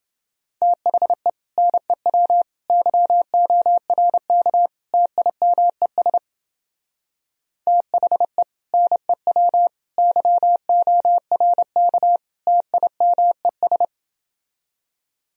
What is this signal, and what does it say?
Morse code, heard as a keyed tone, about 30 wpm, 700 Hz. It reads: THE NEW YORK TIMES THE NEW YORK TIMES